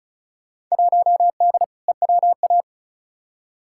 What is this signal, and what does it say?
Morse code, heard as a keyed tone, about 36 wpm, 700 Hz. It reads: 1D EWA
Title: Field Day Exchanges - 35wpm